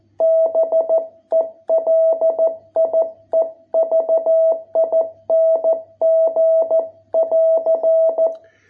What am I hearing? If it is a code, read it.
Morse code, heard as a keyed tone, about 13 wpm, 660 Hz. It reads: BELIEVING <AR>